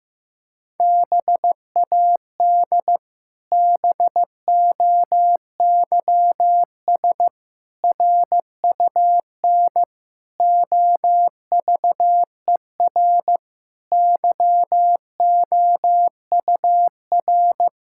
Morse code, heard as a keyed tone, about 16 wpm, 700 Hz. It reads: BAD BOYS RUN OVER YOUR
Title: Pangram Challenge v2 - 15wpm